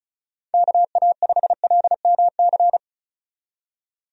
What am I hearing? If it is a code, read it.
Morse code, heard as a keyed tone, about 36 wpm, 700 Hz. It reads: KA5LMC
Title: USA Call Signs - 6 Characters - 35wpm